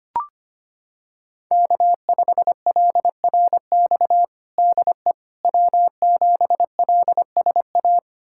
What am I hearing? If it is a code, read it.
Morse code, heard as a keyed tone, about 25 wpm, 700 Hz. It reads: K5LRX DE W7LSA
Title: USA Callsign DE Callsign 25wpm